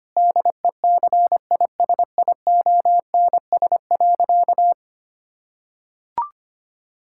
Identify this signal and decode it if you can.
Morse code, heard as a keyed tone, about 25 wpm, 700 Hz. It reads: DECISIONS.
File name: practice / 25wpm / Difficult Sentences with Hints - 25wpm